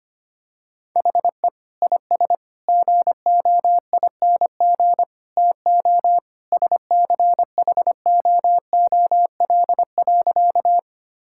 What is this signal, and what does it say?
Morse code, heard as a keyed tone, about 25 wpm, 700 Hz. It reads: HE IS GOING TO SCHOOL.
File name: practice / 25wpm / Sentences with 3-Word Phrases - 25wpm